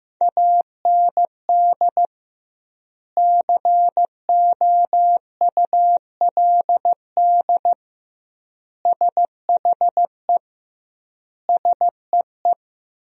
Morse code, heard as a keyed tone, about 16 wpm, 700 Hz. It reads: AND COULD SHE SEE
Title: 2x Word-Spacing - Sentences from Top 100 Words - 15wpm